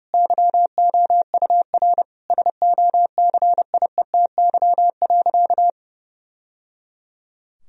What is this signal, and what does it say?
Morse code, heard as a keyed tone, about 32 wpm, 700 Hz. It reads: YOUR SOCIETY.
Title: Sentences from Top 1000 Words - Review Part 4 - 30wpm